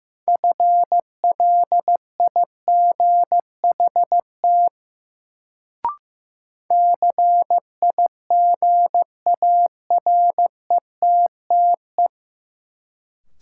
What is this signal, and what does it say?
Morse code, heard as a keyed tone, about 15 wpm, 700 Hz. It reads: FLIGHT CIGARETTE
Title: Sentences from Top 1000 Words - New Word Drill - 15wpm